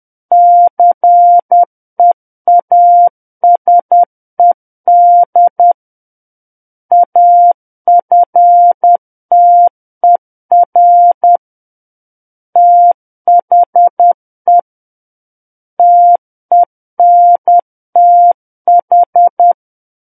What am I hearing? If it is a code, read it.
Morse code, heard as a keyed tone, about 10 wpm, 700 Hz. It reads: CEASED AFTER THE TENTH
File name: Book One / 01-Chapter 1 - 10 WPM